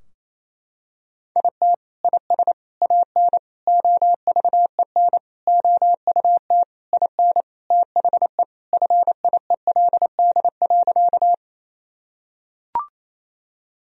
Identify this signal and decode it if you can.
Morse code, heard as a keyed tone, about 28 wpm, 700 Hz. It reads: IT IS AN OVEN OUT IN THE FIELD.
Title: Sentences from Top 1000 Words - Review Part 4 - 28wpm